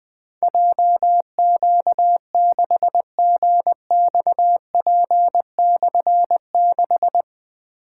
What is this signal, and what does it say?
Morse code, heard as a keyed tone, about 20 wpm, 700 Hz. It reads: JQ6GXP/6
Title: Difficult Call Signs 20wpm